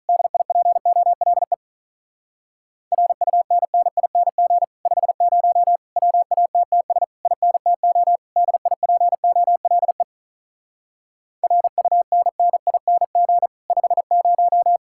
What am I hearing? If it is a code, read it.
Morse code, heard as a keyed tone, about 46 wpm, 700 Hz. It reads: DIPOLE RUNNING 50 WATTS INTO DIPOLE RUNNING 50